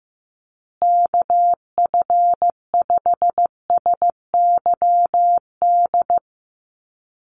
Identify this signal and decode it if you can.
Morse code, heard as a keyed tone, about 15 wpm, 700 Hz. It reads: KF5SYD
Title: USA Call Signs Repeated 2x 15wpm